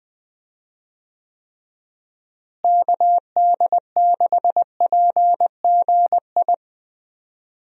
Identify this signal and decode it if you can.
Morse code, heard as a keyed tone, about 21 wpm, 700 Hz. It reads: KD6PGI
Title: USA Call Signs - 6 Characters - Repeated 3x - 20wpm